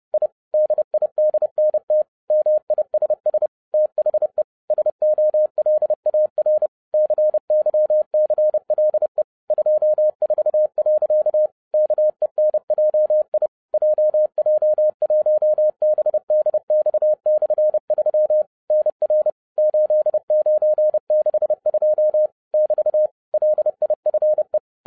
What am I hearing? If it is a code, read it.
Morse code, heard as a keyed tone, about 30 wpm, 610 Hz. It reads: I DIDNT MISS THE SOLAR CYCLE 24. KENJI JJ1BDX/3 NR 8962 <BT> LIFE
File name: 2012-01-mp3 / n11-I DIDNT MISS THE SOLAR CYCLE 24-30wpm